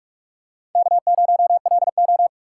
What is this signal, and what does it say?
Morse code, heard as a keyed tone, about 42 wpm, 700 Hz. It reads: K0LO
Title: USA Call Signs - 4 Characters - 4x Character Spacing - 45wpm